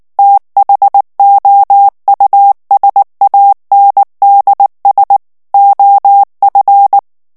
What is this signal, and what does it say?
Morse code, heard as a keyed tone, about 19 wpm, 800 Hz. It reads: THOUSANDS OF
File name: ANGLAIS texte clair 2 vitesse 18 mots minute